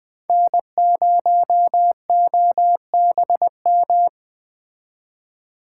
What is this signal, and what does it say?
Morse code, heard as a keyed tone, about 20 wpm, 700 Hz. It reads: N0OBM
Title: Speed Racing - USA 1x3 Call Signs 17wpm